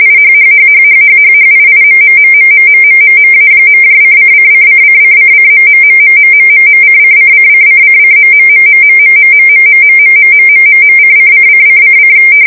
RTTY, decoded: IHHHHHHHHIIMMMMMMMMMA.:::::::::........::::::::.................:::::::